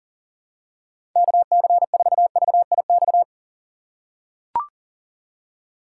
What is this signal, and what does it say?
Morse code, heard as a keyed tone, about 40 wpm, 700 Hz. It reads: KC4VIX
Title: USA Call Signs - 6 Characters - 4x Character Spacing - 40wpm